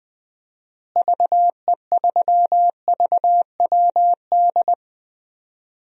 Morse code, heard as a keyed tone, about 21 wpm, 700 Hz. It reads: VE3VWD
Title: DXCC-100 Call Signs - 6 Characters - Repeated 3x - 20wpm